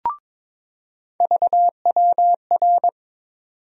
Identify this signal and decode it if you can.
Morse code, heard as a keyed tone, about 22 wpm, 700 Hz. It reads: VWR
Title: USA Call Sign Postfixes - 3 Letters 22wpm